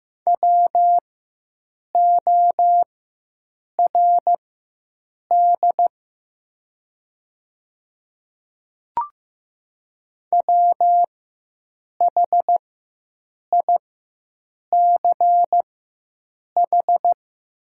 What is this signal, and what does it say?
Morse code, heard as a keyed tone, about 15 wpm, spaced as for 7 wpm, 700 Hz. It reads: WORD WHICH
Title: 4x Character Spacing - Top 100 Words - 15wpm